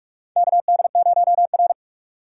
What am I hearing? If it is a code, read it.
Morse code, heard as a keyed tone, about 46 wpm, 700 Hz. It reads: KD0R
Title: USA Call Signs - 4 Characters - 6x Character Spacing - 45wpm